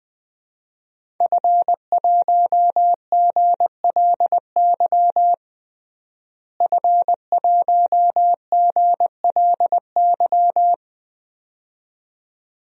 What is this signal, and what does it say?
Morse code, heard as a keyed tone, about 21 wpm, 700 Hz. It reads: F1GLY F1GLY
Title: DXCC-100 Call Signs - 5 Characters - Repeated 2x - 20wpm